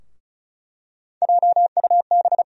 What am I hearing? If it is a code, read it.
Morse code, heard as a keyed tone, about 36 wpm, 700 Hz. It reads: JUB
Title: USA Call Sign Postfixes - 3 Letters 35wpm